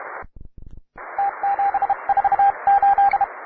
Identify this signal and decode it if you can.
Morse code, heard as a keyed tone, about 31 wpm, 800 Hz, with radio noise. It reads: T748
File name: wae / cw advanced / tu2ci cw